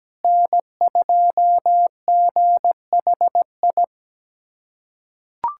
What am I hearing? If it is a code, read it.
Morse code, heard as a keyed tone, about 17 wpm, 700 Hz. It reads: N2GHI
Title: USA Callsign DE Callsign 17wpm